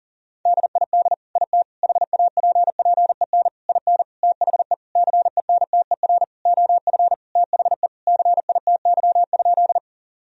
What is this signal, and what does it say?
Morse code, heard as a keyed tone, about 40 wpm, 700 Hz. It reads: DID IT HAPPEN IN THE CENTER OF THE CITY?
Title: Sentences from Top 1000 Words - Review Part 2 - 40wpm